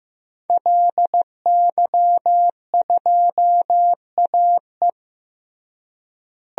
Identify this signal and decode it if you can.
Morse code, heard as a keyed tone, about 16 wpm, 700 Hz. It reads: LY2AE
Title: DXCC-100 Call Signs - 5 Characters - Repeated 2x - 15wpm